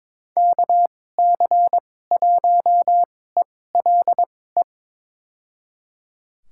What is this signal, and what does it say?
Morse code, heard as a keyed tone, about 21 wpm, spaced as for 16 wpm, 700 Hz. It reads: KC1ELE
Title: USA Call Signs - 6 Characters - 2x Character Spacing - 22wpm